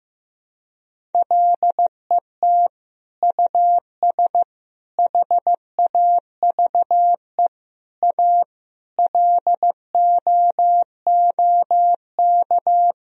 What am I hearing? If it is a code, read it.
Morse code, heard as a keyed tone, about 15 wpm, 700 Hz. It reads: LET US HAVE A LOOK
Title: Sentences from Top 1000 Words - Review Part 1 - 15wpm